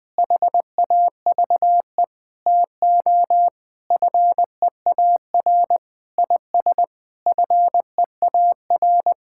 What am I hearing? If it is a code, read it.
Morse code, heard as a keyed tone, about 21 wpm, 700 Hz. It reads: HAVE TO FEAR IS FEAR
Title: Sentences with 8-Word Phrases - 20wpm